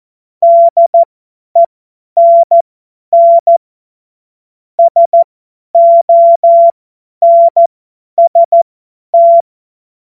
Morse code, heard as a keyed tone, about 14 wpm, spaced as for 10 wpm, 680 Hz. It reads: DENN SONST
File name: Morsekurs / Witze / Witz 07. März 2023 50 F70